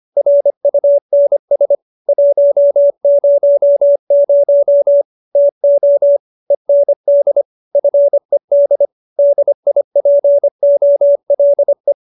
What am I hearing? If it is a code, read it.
Morse code, heard as a keyed tone, about 25 wpm, 570 Hz. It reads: RUNS 100 TO END FED DIPOLE
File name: QSO306 25